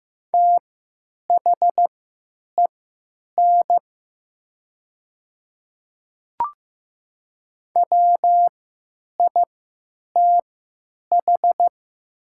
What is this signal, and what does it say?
Morse code, heard as a keyed tone, about 15 wpm, spaced as for 9 wpm, 700 Hz. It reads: THEN WITH
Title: 3x Character Spacing - Top 100 Words - 15wpm